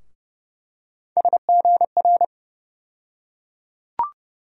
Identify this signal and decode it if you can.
Morse code, heard as a keyed tone, about 30 wpm, 700 Hz. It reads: SGR
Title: USA Call Sign Postfixes - 3 Letters 30wpm